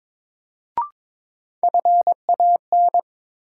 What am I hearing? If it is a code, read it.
Morse code, heard as a keyed tone, about 21 wpm, 700 Hz. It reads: FAN
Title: Sentences from Top 1000 Words - Review Part 4 - New Word Drill - 22wpm